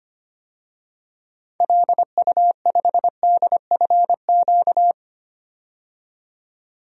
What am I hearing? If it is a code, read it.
Morse code, heard as a keyed tone, about 25 wpm, 700 Hz. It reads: LU5DFQ
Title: DXCC-100 Call Signs - 6 Characters - 25wpm